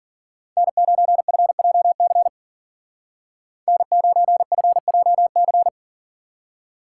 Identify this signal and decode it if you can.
Morse code, heard as a keyed tone, about 49 wpm, 700 Hz. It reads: N9FJC N9FJC
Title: Speed Racing - USA 1x3 Call Signs 40wpm